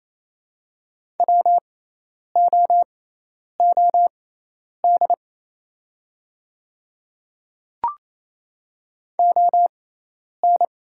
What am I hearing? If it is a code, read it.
Morse code, heard as a keyed tone, about 28 wpm, spaced as for 10 wpm, 690 Hz. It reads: WOOD ON
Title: 6x Character Spacing - Top 500 Words - 28wpm